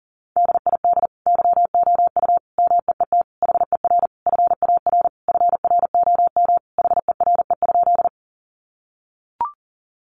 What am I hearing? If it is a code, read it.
Morse code, heard as a keyed tone, about 40 wpm, 700 Hz. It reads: DID YOU MEET HER FAR FROM HERE?
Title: Sentences from Top 1100 Words 40wpm